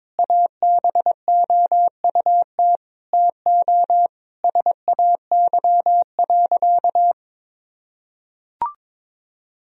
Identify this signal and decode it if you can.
Morse code, heard as a keyed tone, about 22 wpm, 700 Hz. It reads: ABOUT TO SAY.
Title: Sentences from Top 1000 Words - Review Part 2 - 22wpm